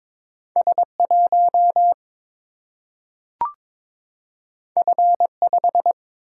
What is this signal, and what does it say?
Morse code, heard as a keyed tone, about 22 wpm, 700 Hz. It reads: S 1 F 5
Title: Single-Letter Single-Number 22wpm